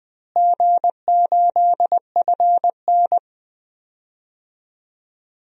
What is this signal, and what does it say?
Morse code, heard as a keyed tone, about 20 wpm, 700 Hz. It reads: G8FN
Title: DXCC-100 Call Signs - 4 Characters - Repeated 3x - 20wpm